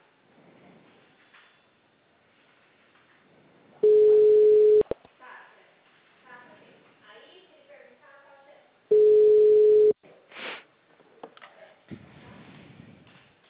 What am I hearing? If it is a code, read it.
Ringback tone (Europe, 425 Hz), with speech.